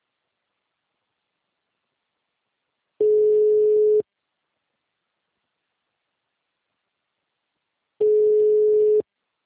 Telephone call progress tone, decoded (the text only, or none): ringback tone